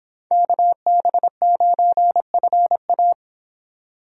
Morse code, heard as a keyed tone, about 25 wpm, 700 Hz. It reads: KB9FA